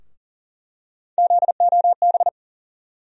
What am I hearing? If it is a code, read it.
Morse code, heard as a keyed tone, about 40 wpm, 700 Hz. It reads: ZOB
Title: USA Call Sign Postfixes - 3 Letters 40wpm